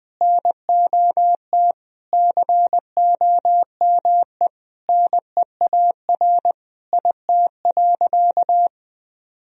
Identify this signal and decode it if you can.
Morse code, heard as a keyed tone, about 20 wpm, 700 Hz. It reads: NOT COME NEAR IT.
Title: Sentences from Top 1000 Words - Review Part 1 - 20wpm